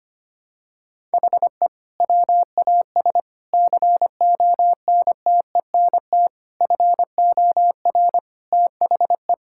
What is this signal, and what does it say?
Morse code, heard as a keyed tone, about 25 wpm, 700 Hz. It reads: HE WAS CONTENT FOR THE
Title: Sentences from Top 1000 Words - Review Part 6 - 25wpm